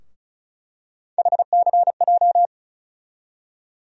Morse code, heard as a keyed tone, about 36 wpm, 700 Hz. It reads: HCJ
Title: USA Call Sign Postfixes - 3 Letters 35wpm